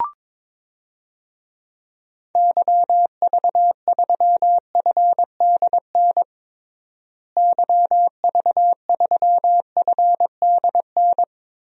Morse code, heard as a keyed tone, about 22 wpm, 700 Hz. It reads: YV3FDN YV3FDN
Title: DXCC-100 Call Signs - 6 Characters - Repeated 3x - 22wpm